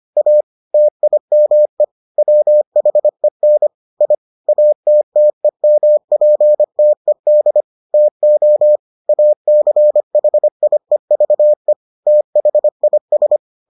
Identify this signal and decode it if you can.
Morse code, heard as a keyed tone, about 25 wpm, 600 Hz. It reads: A TIME WHEN I ATTEMPTED TO ACHIEVE THIS